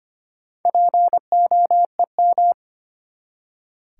Morse code, heard as a keyed tone, about 25 wpm, 700 Hz. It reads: POEM
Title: Four Letter Words - 25wpm